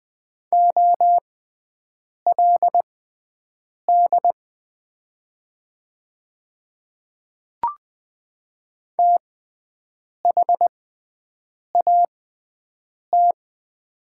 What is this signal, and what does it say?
Morse code, heard as a keyed tone, about 20 wpm, spaced as for 7 wpm, 700 Hz. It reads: OLD THAT
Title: 6x Character Spacing - Top 300 Words - 20wpm